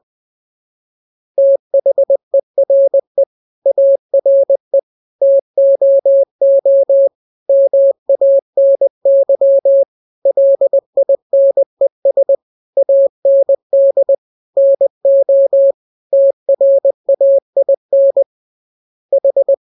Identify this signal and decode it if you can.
Morse code, heard as a keyed tone, about 20 wpm, 550 Hz. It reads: THERE ARE TOO MANY LINES AND NO TRAIN H